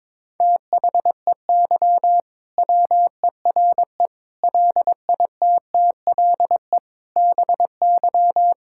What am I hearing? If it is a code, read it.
Morse code, heard as a keyed tone, about 22 wpm, 700 Hz. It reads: THEY WERE LITTLE BY
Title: Sentences from Top 1600 Words 22wpm